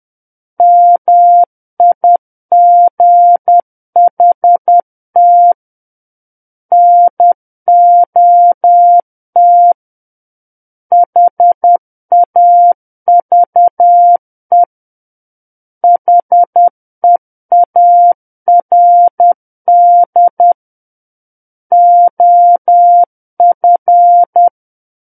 Morse code, heard as a keyed tone, about 10 wpm, 700 Hz. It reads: MIGHT NOT HAVE HEARD OF
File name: Book One / 01-Chapter 1 - 10 WPM